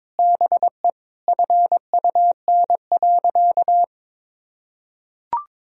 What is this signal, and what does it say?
Morse code, heard as a keyed tone, about 22 wpm, 700 Hz. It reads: BE FUN.
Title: Sentences from Top 1000 Words - Review Part 4 - 22wpm